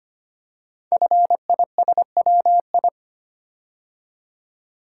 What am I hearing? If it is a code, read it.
Morse code, heard as a keyed tone, about 25 wpm, 700 Hz. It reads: FISWI
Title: Code Groups - 5 Letters - 25wpm